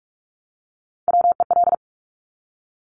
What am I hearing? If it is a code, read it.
Morse code, heard as a keyed tone, about 46 wpm, 700 Hz. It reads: WEL
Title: Top Three Letter Combinations in Words 45wpm